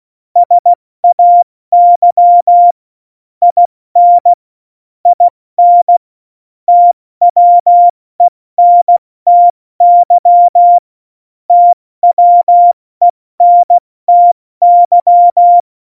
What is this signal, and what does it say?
Morse code, heard as a keyed tone, about 16 wpm, 700 Hz. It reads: SAY IN IN TWENTY TWENTY
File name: Morsekurs / Witze / Witz 25 Juli 17, Tempo 70 F80